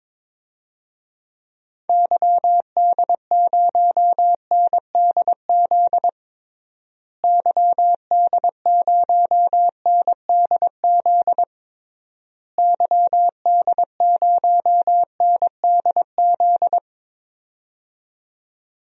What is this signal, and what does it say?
Morse code, heard as a keyed tone, about 22 wpm, 700 Hz. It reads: YD0NDZ YD0NDZ YD0NDZ
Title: DXCC-100 Call Signs - 6 Characters - Repeated 3x - 22wpm